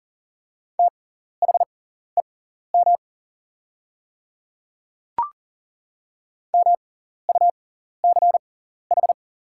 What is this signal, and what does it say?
Morse code, heard as a keyed tone, about 40 wpm, spaced as for 14 wpm, 690 Hz. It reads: THEM MUCH
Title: 6x Character Spacing - Top 300 Words - 40wpm